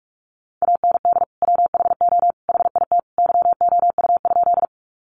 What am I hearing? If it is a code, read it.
Morse code, heard as a keyed tone, about 44 wpm, 700 Hz. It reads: AND WHO HIT YOU?
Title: Sentences from Top 1200 Words 45wpm